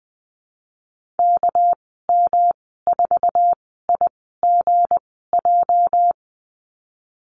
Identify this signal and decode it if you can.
Morse code, heard as a keyed tone, about 20 wpm, spaced as for 15 wpm, 700 Hz. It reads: KM4IGJ
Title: 219 Lesson 35g - Number 0 - Call Signs 20wpm